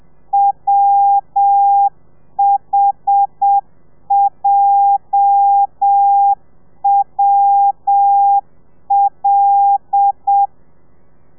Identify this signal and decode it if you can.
Morse code, heard as a keyed tone, about 7 wpm, 800 Hz. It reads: WHJWL
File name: telegrafia / i2tfj / Lez8 30